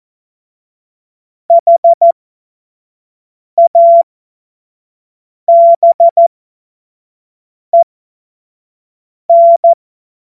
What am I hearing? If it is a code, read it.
Morse code, heard as a keyed tone, about 14 wpm, spaced as for 5 wpm, 680 Hz. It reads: HABEN
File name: Witz-15-Juli-2025-25-F70